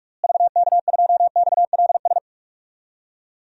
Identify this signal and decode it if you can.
Morse code, heard as a keyed tone, about 46 wpm, 700 Hz. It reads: VK2XLS
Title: DXCC-100 Call Signs - 6 Characters - Repeated 3x - 45wpm